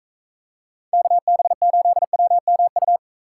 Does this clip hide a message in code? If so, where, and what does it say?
Yes: Morse code from 0.9 to 3.0 s, heard as a keyed tone, about 45 wpm, 700 Hz. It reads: KB8WMU